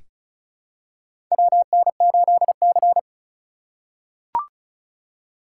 Morse code, heard as a keyed tone, about 36 wpm, 700 Hz. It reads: WN8C
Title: USA Call Signs - 4 Characters - 3x Character Spacing - 35wpm